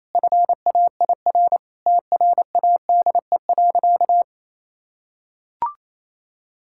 Morse code, heard as a keyed tone, about 28 wpm, 700 Hz. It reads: FAIR TRADE.
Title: Sentences from Top 1600 Words 28wpm